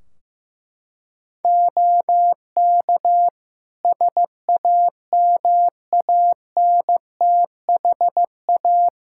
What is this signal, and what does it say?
Morse code, heard as a keyed tone, about 15 wpm, 700 Hz. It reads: OK SAMANTHA
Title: Practice QSO - Exchange 2 - Info - 15wpm